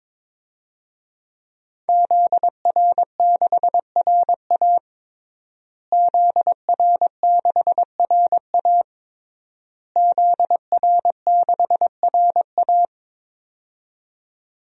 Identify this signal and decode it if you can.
Morse code, heard as a keyed tone, about 22 wpm, 700 Hz. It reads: ZR6RA ZR6RA ZR6RA
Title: DXCC-100 Call Signs - 5 Characters - Repeated 3x - 22wpm